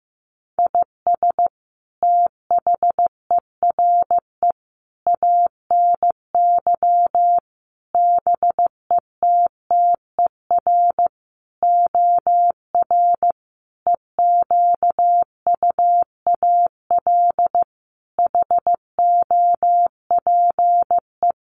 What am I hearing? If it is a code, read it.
Morse code, heard as a keyed tone, about 15 wpm, 700 Hz. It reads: IS THERE ANY BETTER OR EQUAL HOPE
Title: Sentences from Top 1400 Words 15wpm